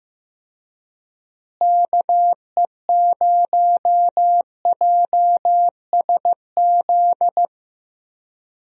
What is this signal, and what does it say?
Morse code, heard as a keyed tone, about 15 wpm, 700 Hz. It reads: KE0JSZ
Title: USA Call Signs - 6 Characters - Repeated 2x - 15wpm